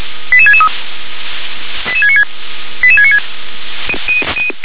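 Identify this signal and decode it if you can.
Five-tone selective call (ZVEI), heard as quick runs of five tones, tones of about 70 ms. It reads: 88663 88696 88696
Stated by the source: ZVEI five-tone selective call received on an SDR.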